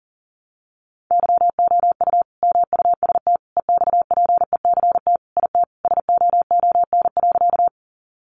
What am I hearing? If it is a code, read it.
Morse code, heard as a keyed tone, about 40 wpm, 700 Hz. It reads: YOU MUST EXPECT IT SOON.
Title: Sentences from Top 1100 Words 40wpm